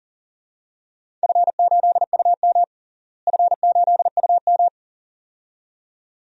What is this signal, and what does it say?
Morse code, heard as a keyed tone, about 40 wpm, 700 Hz. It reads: F8UM F8UM
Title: DXCC-100 Call Signs - 4 Characters - Repeated 2x - 40wpm